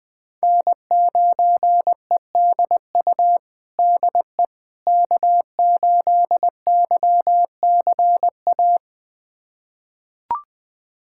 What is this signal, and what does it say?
Morse code, heard as a keyed tone, about 21 wpm, 700 Hz. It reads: N9EDU DE K8YCA
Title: USA Callsign DE Callsign - 5 Characters - 20wpm